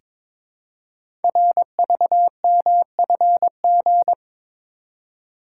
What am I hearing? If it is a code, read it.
Morse code, heard as a keyed tone, about 22 wpm, 700 Hz. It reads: RVMFG
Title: Code Groups - 5 Letter:Numbers - 22wpm